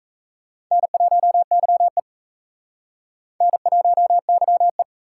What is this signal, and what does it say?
Morse code, heard as a keyed tone, about 35 wpm, 700 Hz. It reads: N1YE N1YE